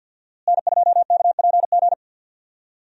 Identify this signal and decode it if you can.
Morse code, heard as a keyed tone, about 52 wpm, 700 Hz. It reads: N2KPG
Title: USA Call Signs - 5 Characters - Repeated 2x - 50wpm